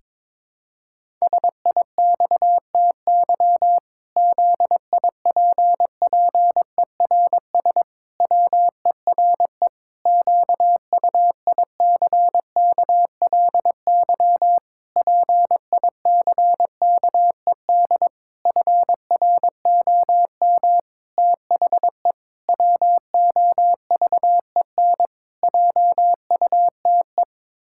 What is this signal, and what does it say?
Morse code, heard as a keyed tone, about 22 wpm, 700 Hz. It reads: SIXTY ZIPPERS WERE QUICKLY PICKED FROM THE WOVEN JUTE
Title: Pangram Challenge v2 - 22wpm